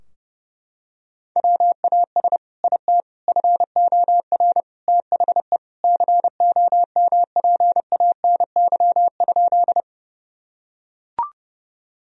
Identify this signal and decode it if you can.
Morse code, heard as a keyed tone, about 30 wpm, 700 Hz. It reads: WAS IT FOR THE COMPANY?
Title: Sentences from Top 1000 Words - Review Part 3 - 30wpm